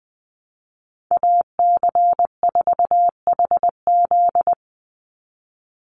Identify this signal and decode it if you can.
Morse code, heard as a keyed tone, about 20 wpm, 700 Hz. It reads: AC4HZ
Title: USA Call Signs 20wpm